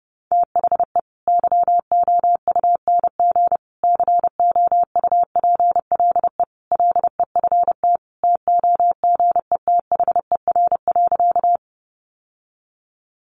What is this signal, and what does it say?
Morse code, heard as a keyed tone, about 30 wpm, 700 Hz. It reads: THE YOUNG COUPLE LEFT TOGETHER.
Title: Sentences from Top 1100 Words 30wpm